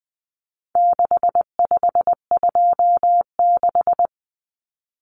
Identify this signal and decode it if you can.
Morse code, heard as a keyed tone, about 20 wpm, 700 Hz. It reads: 6526
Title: Numbers - 4-Digit 20wpm